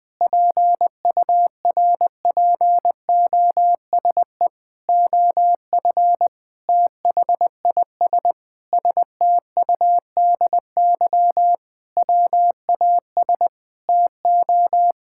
Morse code, heard as a keyed tone, about 20 wpm, 700 Hz. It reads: PURPOSE OF THIS STUDY WAS TO
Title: Sentences with 5-Word Phrases - 20wpm